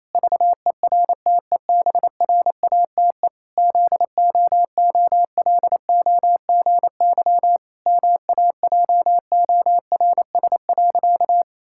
Morse code, heard as a keyed tone, about 28 wpm, 700 Hz. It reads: VERTEBRATE ZOOLOGY MAJORS.
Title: Pangram Challenge v2 - 28wpm